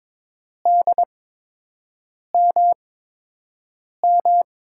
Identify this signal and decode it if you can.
Morse code, heard as a keyed tone, about 22 wpm, spaced as for 6 wpm, 700 Hz. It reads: DMM
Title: USA Call Signs - 6 Characters - 8x Character Spacing - 22wpm